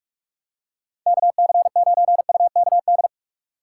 Morse code, heard as a keyed tone, about 45 wpm, 700 Hz. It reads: KC9UKD
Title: USA Call Signs - 3x Character Spacing - 45wpm